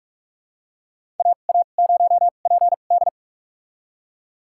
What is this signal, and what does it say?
Morse code, heard as a keyed tone, about 44 wpm, spaced as for 32 wpm, 700 Hz. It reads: AA0PD
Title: USA Call Signs - 5 Characters - 2x Character Spacing - 45wpm